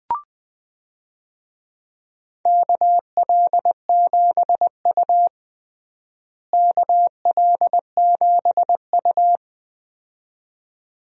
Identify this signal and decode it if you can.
Morse code, heard as a keyed tone, about 20 wpm, 700 Hz. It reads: KL7U KL7U
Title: DXCC-100 Call Signs - 4 Characters - Repeated 2x - 20wpm